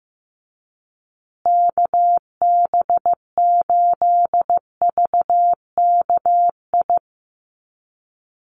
Morse code, heard as a keyed tone, about 15 wpm, 700 Hz. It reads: KB8VKI
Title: USA Call Signs Repeated 3x 15wpm